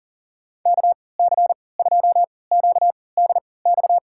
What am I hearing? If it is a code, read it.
Morse code, heard as a keyed tone, about 40 wpm, spaced as for 23 wpm, 700 Hz. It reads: KC2QDX
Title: USA Call Signs - 6 Characters - 3x Character Spacing - 40wpm